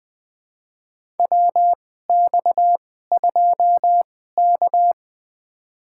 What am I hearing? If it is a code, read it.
Morse code, heard as a keyed tone, about 20 wpm, spaced as for 14 wpm, 700 Hz. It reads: WX2K
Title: USA Call Signs - 4 Characters - 2x Character Spacing - 20wpm